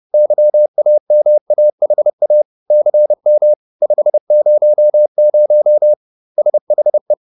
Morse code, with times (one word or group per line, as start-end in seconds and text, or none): 0.1-2.4 YAMAHA
2.7-3.5 CM
3.8-5.9 500
6.4-7.1 SHE